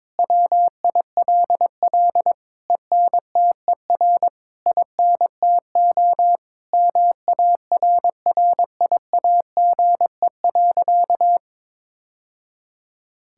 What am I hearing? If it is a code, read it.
Morse code, heard as a keyed tone, about 22 wpm, 700 Hz. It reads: WILL ENTER INTO MARRIAGE.
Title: Sentences from Top 1000 Words - Review Part 6 - 22wpm